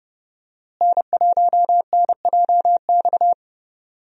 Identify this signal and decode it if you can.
Morse code, heard as a keyed tone, about 30 wpm, 700 Hz. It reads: N1NJX
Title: USA Call Signs - 5 Characters - Repeated 2x - 30wpm